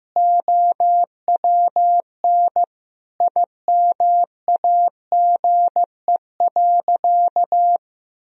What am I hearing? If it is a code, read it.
Morse code, heard as a keyed tone, about 15 wpm, 700 Hz. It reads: OWN IMAGE.
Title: Sentences from Top 1000 Words - Review Part 4 - 15wpm